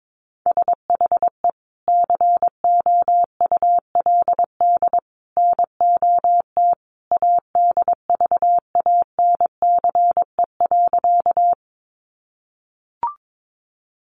Morse code, heard as a keyed tone, about 22 wpm, 700 Hz. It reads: SHE COULD NOT ADVANCE.
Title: Sentences from Top 1400 Words 22wpm